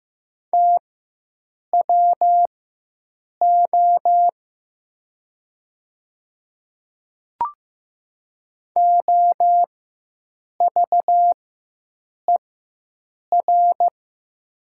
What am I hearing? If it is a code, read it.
Morse code, heard as a keyed tone, about 15 wpm, spaced as for 7 wpm, 690 Hz. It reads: TWO OVER
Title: 4x Character Spacing - Top 200 Words - 15wpm